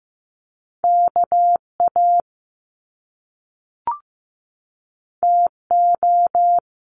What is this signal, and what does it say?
Morse code, heard as a keyed tone, about 15 wpm, 700 Hz. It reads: KA TO
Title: Top Two Letter Combinations in Words 15wpm